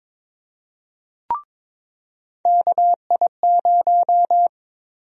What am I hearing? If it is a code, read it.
Morse code, heard as a keyed tone, about 22 wpm, 700 Hz. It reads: KI0
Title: USA Call Sign Prefixes - 22wpm